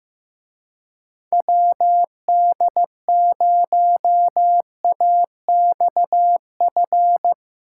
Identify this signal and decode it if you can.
Morse code, heard as a keyed tone, about 17 wpm, 700 Hz. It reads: WD0AXF
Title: Speed Racing - USA Call Signs 15wpm